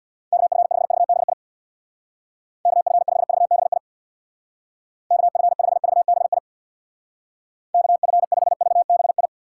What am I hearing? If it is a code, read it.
Morse code, heard as a keyed tone, about 66 wpm, 700 Hz. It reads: KF5VBI KF5VBI KF5VBI KF5VBI